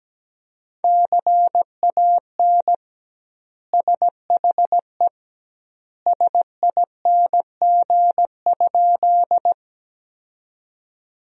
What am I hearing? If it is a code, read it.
Morse code, heard as a keyed tone, about 17 wpm, 700 Hz. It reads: CAN SHE SING?
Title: 2x Word-Spacing - Sentences from Top 400 Words - 17wpm